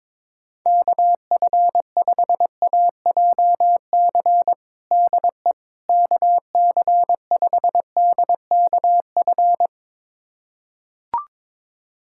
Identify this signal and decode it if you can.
Morse code, heard as a keyed tone, about 22 wpm, 700 Hz. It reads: KF5AJC DE KC5DKF
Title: USA Callsign DE Callsign - 6 Characters - 22wpm